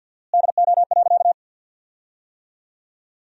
Morse code, heard as a keyed tone, about 47 wpm, 700 Hz. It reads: DO.
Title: 4x Word-Spacing - Sentences from Top 100 Words - 50wpm